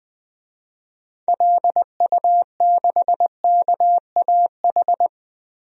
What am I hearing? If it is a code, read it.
Morse code, heard as a keyed tone, about 20 wpm, 700 Hz. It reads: LU6KAH
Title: DXCC-100 Call Signs - 6 Characters - Repeated 2x - 20wpm